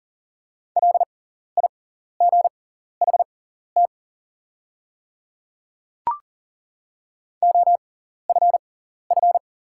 Morse code, heard as a keyed tone, about 40 wpm, spaced as for 14 wpm, 700 Hz. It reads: LIGHT OFF
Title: 6x Character Spacing - Top 300 Words - 40wpm